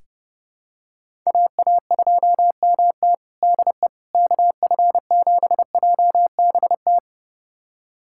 Morse code, heard as a keyed tone, about 30 wpm, 700 Hz. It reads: AA2MT DE KF7JBT
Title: USA Callsign DE Callsign 30wpm